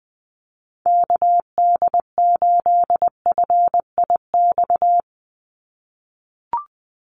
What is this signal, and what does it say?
Morse code, heard as a keyed tone, about 20 wpm, 700 Hz. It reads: KD8FIX
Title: USA Call Signs 20wpm